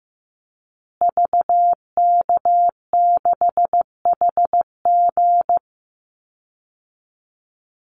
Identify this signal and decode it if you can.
Morse code, heard as a keyed tone, about 14 wpm, 700 Hz. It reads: VK6HG
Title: Call Signs Repeated 2x - 15wpm